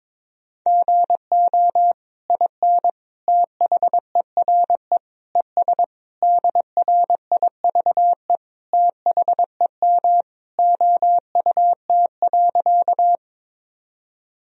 Morse code, heard as a keyed tone, about 21 wpm, 700 Hz. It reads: GO IN THERE ES DRIVE THEM OUT.
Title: Sentences from Top 500 Words and Abbreviations 22wpm